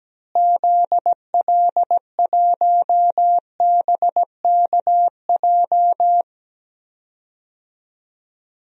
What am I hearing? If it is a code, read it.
Morse code, heard as a keyed tone, about 17 wpm, 700 Hz. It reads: ZL1BKJ
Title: DXCC-100 Call Signs - 6 Characters - Repeated 2x - 17wpm